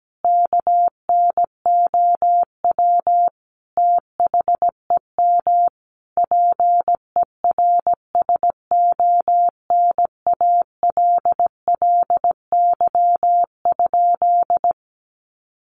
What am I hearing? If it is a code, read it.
Morse code, heard as a keyed tone, about 17 wpm, 700 Hz. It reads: KNOW THEM PERSONALLY?
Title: Sentences from Top 1400 Words 17wpm